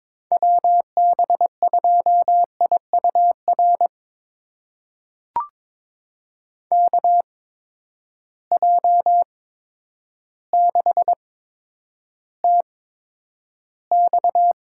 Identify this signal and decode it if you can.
Morse code, heard as a keyed tone, about 21 wpm, spaced as for 10 wpm, 700 Hz. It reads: WB2IUR KJ6TX
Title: USA Call Signs - 8x Character Spacing - 22wpm